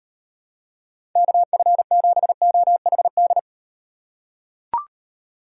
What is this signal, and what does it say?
Morse code, heard as a keyed tone, about 38 wpm, 700 Hz. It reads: KF7OHD